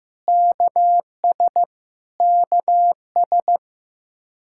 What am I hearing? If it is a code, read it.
Morse code, heard as a keyed tone, about 15 wpm, 700 Hz. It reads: KS KS
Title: Practice QSO - Exchange 1 - Beginning - 15wpm